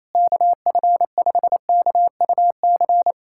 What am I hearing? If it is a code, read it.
Morse code, heard as a keyed tone, about 28 wpm, 700 Hz. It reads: KF5KUC
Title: USA Callsign DE Callsign 28wpm